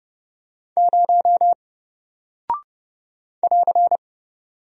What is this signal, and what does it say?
Morse code, heard as a keyed tone, about 30 wpm, 700 Hz. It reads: 0 <AR>
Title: Single Character (Letter:Number:Punctuation:Prosign) 30wpm